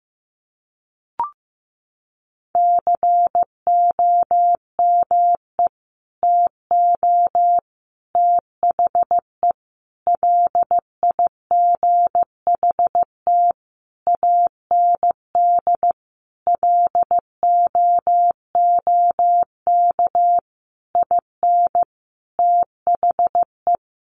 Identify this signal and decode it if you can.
Morse code, heard as a keyed tone, about 15 wpm, 700 Hz. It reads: COME TO THE LIGHT AND LOOK IN THE
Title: Sentences from Top 1300 Words 15wpm